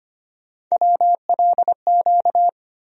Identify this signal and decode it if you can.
Morse code, heard as a keyed tone, about 25 wpm, 700 Hz. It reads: WLQ
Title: USA Call Sign Postfixes - 3 Letters 25wpm